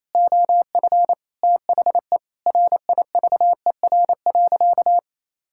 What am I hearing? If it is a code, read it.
Morse code, heard as a keyed tone, about 28 wpm, 700 Hz. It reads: OF THE RIVER.
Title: Sentences from Top 1000 Words - Review Part 6 - 28wpm